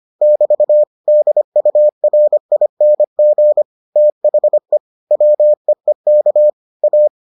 Morse code, heard as a keyed tone, about 25 wpm, 600 Hz. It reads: <BT> DURING THE WEEK A